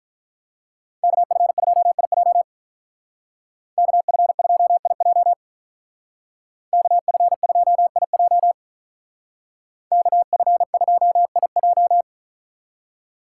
Morse code, heard as a keyed tone, about 43 wpm, 700 Hz. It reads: KF2IJ KF2IJ KF2IJ KF2IJ
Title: Speed Racing - USA 2x2 Call Signs 35wpm